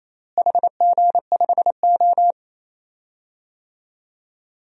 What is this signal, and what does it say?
Morse code, heard as a keyed tone, about 28 wpm, 700 Hz. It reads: HG5O
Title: DXCC-100 Call Signs - 4 Characters - Repeated 2x - 28wpm